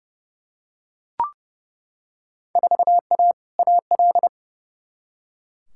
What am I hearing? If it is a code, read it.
Morse code, heard as a keyed tone, about 32 wpm, 700 Hz. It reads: 4A AL
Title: Field Day Exchanges - 30wpm